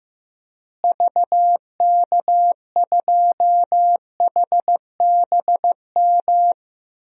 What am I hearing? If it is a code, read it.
Morse code, heard as a keyed tone, about 15 wpm, 700 Hz. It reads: VK2HBM
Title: DXCC-100 Call Signs - 6 Characters - Repeated 3x - 15wpm